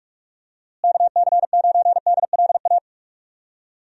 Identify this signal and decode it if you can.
Morse code, heard as a keyed tone, about 46 wpm, 700 Hz. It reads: KC9DLA
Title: USA Call Signs - 6 Characters - Repeated 3x - 45wpm